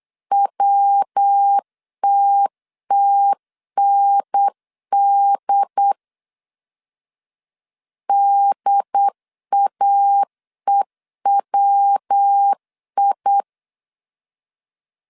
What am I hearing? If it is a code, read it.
Morse code, heard as a keyed tone, about 8 wpm, 800 Hz. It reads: WTTND DAEWI